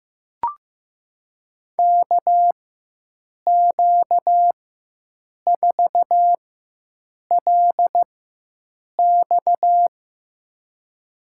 Morse code, heard as a keyed tone, about 15 wpm, spaced as for 7 wpm, 700 Hz. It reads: KQ4LX
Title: USA Call Signs - 4x Character Spacing - 15wpm